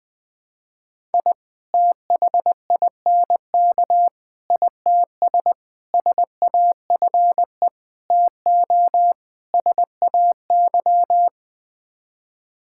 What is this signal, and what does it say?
Morse code, heard as a keyed tone, about 21 wpm, 700 Hz. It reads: I THINK ITS SAFE TO SAY
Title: Sets of 6 Words - 20wpm